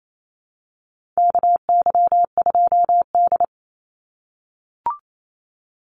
Morse code, heard as a keyed tone, about 28 wpm, 700 Hz. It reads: KY2D
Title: Call Signs - 28wpm